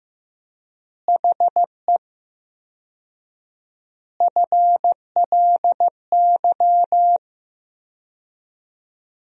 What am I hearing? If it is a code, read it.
Morse code, heard as a keyed tone, about 15 wpm, 700 Hz. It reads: HE FLY
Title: 4x Word-Spacing - Sentences from Top 500 Words - 15wpm